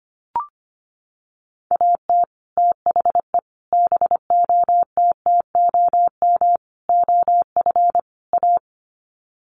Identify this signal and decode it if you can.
Morse code, heard as a keyed tone, about 25 wpm, 700 Hz. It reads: AT THE BOTTOM OF A
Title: Sets of 5 Words Encore2 25wpm